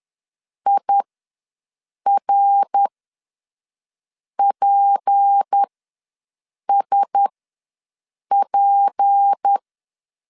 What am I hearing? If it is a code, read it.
Morse code, heard as a keyed tone, about 10 wpm, spaced as for 5 wpm, 800 Hz. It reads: IRPSP
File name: LEZIONI / 6 PKX0 / 6 P ws 10wpm 3751